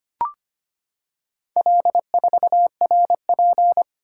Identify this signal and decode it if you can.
Morse code, heard as a keyed tone, about 25 wpm, 700 Hz. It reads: L4RP
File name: practice / 25wpm / Four Letter-Number Combinations - 25wpm